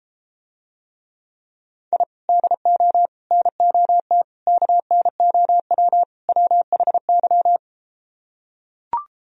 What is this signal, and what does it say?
Morse code, heard as a keyed tone, about 32 wpm, 700 Hz. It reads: I DO NOT KNOW WHY